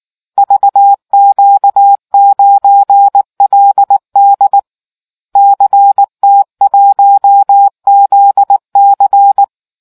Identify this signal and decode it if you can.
Morse code, heard as a keyed tone, about 19 wpm, 800 Hz. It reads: VQ9LD CT1ZC